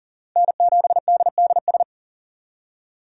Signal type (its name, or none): Morse code